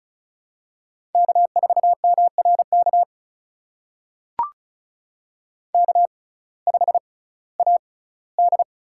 Morse code, heard as a keyed tone, about 36 wpm, spaced as for 19 wpm, 700 Hz. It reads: K4MRK K5AD
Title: USA Call Signs - 5 Characters - 6x Character Spacing - 35wpm